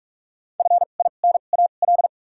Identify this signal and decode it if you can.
Morse code, heard as a keyed tone, about 44 wpm, spaced as for 32 wpm, 700 Hz. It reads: FINAL
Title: 2x Character Spacing - Top 500 Words - 45wpm